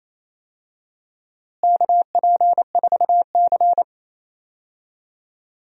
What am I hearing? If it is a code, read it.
Morse code, heard as a keyed tone, about 28 wpm, 700 Hz. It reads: KP4C
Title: DXCC-100 Call Signs - 4 Characters - Repeated 3x - 28wpm